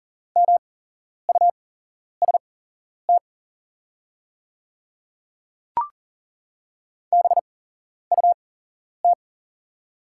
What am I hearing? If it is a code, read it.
Morse code, heard as a keyed tone, about 40 wpm, spaced as for 11 wpm, 700 Hz. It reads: MUST BUT
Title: 8x Character Spacing - Top 300 Words - 40wpm